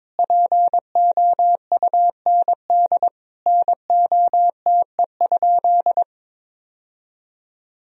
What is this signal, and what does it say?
Morse code, heard as a keyed tone, about 22 wpm, 700 Hz. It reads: POUND NOTE?
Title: Sentences from Top 1000 Words - Review Part 2 - 22wpm